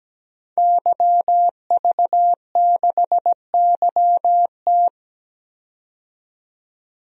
Morse code, heard as a keyed tone, about 17 wpm, 700 Hz. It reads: YV6YT
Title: DXCC-100 Call Signs - 5 Characters - 17wpm